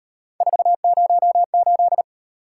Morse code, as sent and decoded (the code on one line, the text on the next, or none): ....- ----- ---..
408